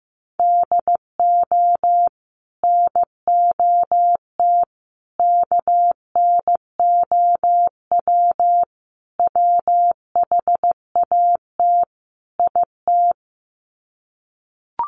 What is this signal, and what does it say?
Morse code, heard as a keyed tone, about 15 wpm, 700 Hz. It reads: DO NOT KNOW WHAT IT
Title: Sets of 5 Words 15wpm